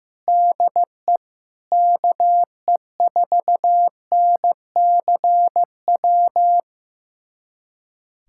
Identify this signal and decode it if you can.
Morse code, heard as a keyed tone, about 15 wpm, 700 Hz. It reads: DE KE4NCW
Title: USA Callsign DE Callsign - 6 Characters - 15wpm